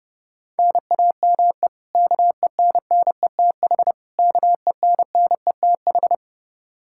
Morse code, heard as a keyed tone, about 32 wpm, 700 Hz. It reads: NAME KENNETH KENNETH
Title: Practice QSO - Exchange 1 - Beginning - 30wpm